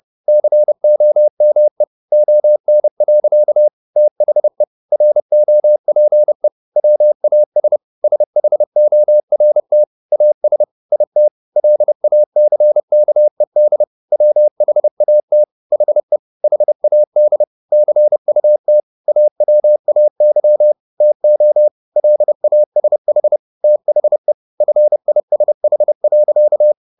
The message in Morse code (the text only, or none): COME ON. THE ROPE WAS SHORT AS IT LACKED WHAT HE HAD CUT AWAY TO LASH THE FISH.